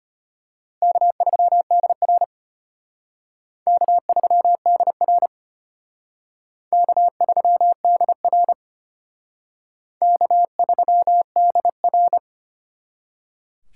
Morse code, heard as a keyed tone, about 34 wpm, 700 Hz. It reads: K3DR K3DR K3DR K3DR